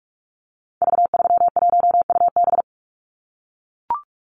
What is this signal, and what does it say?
Morse code, heard as a keyed tone, about 44 wpm, 700 Hz. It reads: V31UB
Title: Call Signs - 45wpm